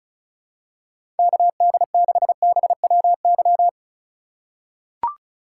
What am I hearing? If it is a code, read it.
Morse code, heard as a keyed tone, about 36 wpm, 700 Hz. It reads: KD6BWY
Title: USA Call Signs - 6 Characters - 4x Character Spacing - 35wpm